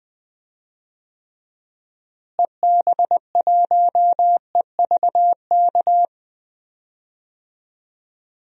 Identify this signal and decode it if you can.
Morse code, heard as a keyed tone, about 20 wpm, 700 Hz. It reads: EB1EVK
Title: DXCC-100 Call Signs - 6 Characters - Repeated 2x - 20wpm